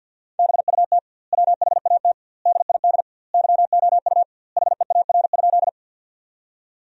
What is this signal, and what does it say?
Morse code, heard as a keyed tone, about 51 wpm, 700 Hz. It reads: BUT WHAT DID YOU HEAR?
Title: 2x Word-Spacing - Sentences from Top 300 Words - 50wpm